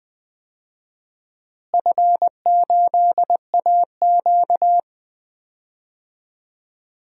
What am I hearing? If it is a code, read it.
Morse code, heard as a keyed tone, about 21 wpm, 700 Hz. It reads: F8AQ
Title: DXCC-100 Call Signs - 4 Characters - Repeated 2x - 20wpm